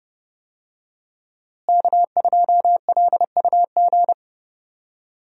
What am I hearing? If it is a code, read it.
Morse code, heard as a keyed tone, about 30 wpm, 700 Hz. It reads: K2LUG
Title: USA Call Signs - 5 Characters - Repeated 3x - 30wpm